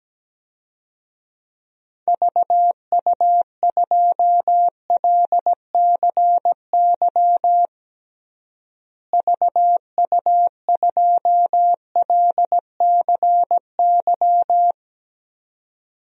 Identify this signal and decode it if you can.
Morse code, heard as a keyed tone, about 17 wpm, 700 Hz. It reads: VU2LCY VU2LCY
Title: DXCC-100 Call Signs - 6 Characters - Repeated 2x - 17wpm